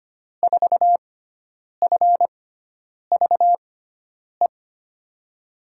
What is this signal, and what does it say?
Morse code, heard as a keyed tone, about 25 wpm, spaced as for 9 wpm, 700 Hz. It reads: 4FVE
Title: USA Call Signs - 6 Characters - 6x Character Spacing - 25wpm